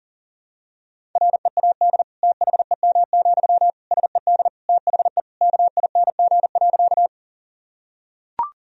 Morse code, heard as a keyed tone, about 40 wpm, 700 Hz. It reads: READ THEM, SED THE KING.
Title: Sentences from Top 400 Words and Abbreviations 40wpm